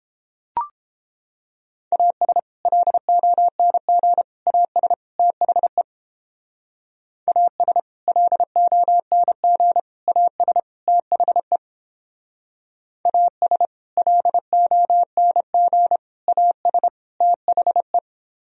Morse code, heard as a keyed tone, about 31 wpm, 700 Hz. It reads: AS LONG AS THE AS LONG AS THE AS LONG AS THE